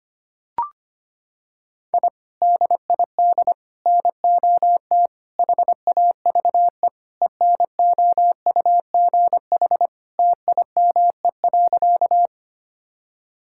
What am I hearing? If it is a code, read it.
Morse code, heard as a keyed tone, about 25 wpm, 700 Hz. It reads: I DID NOT HAVE ENOUGH TIME.
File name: practice / 25wpm / Sentences with 4-Word Phrases - 25wpm